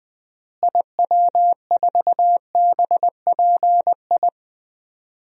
Morse code, heard as a keyed tone, about 20 wpm, 700 Hz. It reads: IW4BPI
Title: DXCC-100 Call Signs - 6 Characters - Repeated 2x - 20wpm